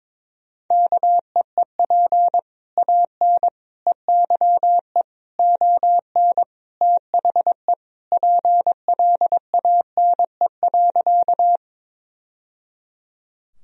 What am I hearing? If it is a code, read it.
Morse code, heard as a keyed tone, about 22 wpm, 700 Hz. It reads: KEEP AN EYE ON THE PLANE.
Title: Sentences from Top 1000 Words - Review Part 3 - 22wpm